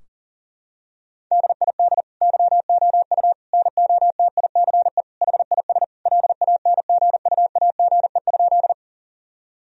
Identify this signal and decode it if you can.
Morse code, heard as a keyed tone, about 37 wpm, 700 Hz. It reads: DID YOU NOTICE HIS LANGUAGE?
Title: Sentences from Top 1000 Words - Review Part 3 - 40wpm